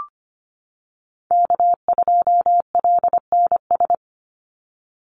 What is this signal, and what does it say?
Morse code, heard as a keyed tone, about 25 wpm, 700 Hz. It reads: K2LNS
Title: Call Signs - 25wpm